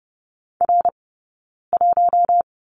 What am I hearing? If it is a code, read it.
Morse code, heard as a keyed tone, about 30 wpm, 700 Hz. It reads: R 1
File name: Lesson 18d - Letter m - Two Characters 30wpm